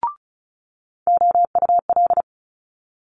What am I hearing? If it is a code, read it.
Morse code, heard as a keyed tone, about 36 wpm, 700 Hz. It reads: OUL
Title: Multi-letter Phonemes Quick Fire 35wpm